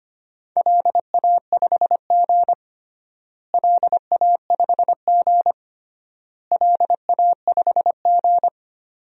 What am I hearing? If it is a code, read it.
Morse code, heard as a keyed tone, about 25 wpm, 700 Hz. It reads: LA5G LA5G LA5G
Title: DXCC-100 Call Signs - 4 Characters - Repeated 3x - 25wpm